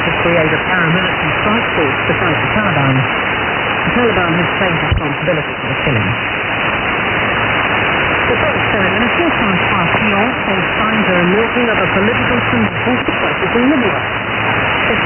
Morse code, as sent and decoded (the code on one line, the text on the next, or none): none